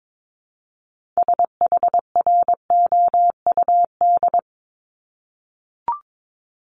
Morse code, heard as a keyed tone, about 22 wpm, 700 Hz. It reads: SHROUD
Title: Multi-letter Phonemes 22wpm